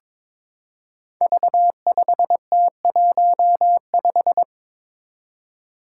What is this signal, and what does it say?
Morse code, heard as a keyed tone, about 23 wpm, 700 Hz. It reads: V5T15
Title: Five Challenging Letter-Number Combinations 22wpm